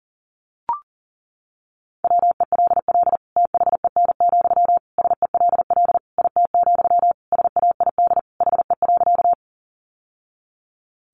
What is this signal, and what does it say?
Morse code, heard as a keyed tone, about 40 wpm, 700 Hz. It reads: WELL THEN, SELL IT, SAID HE.
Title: Sentences from Top 1100 Words 40wpm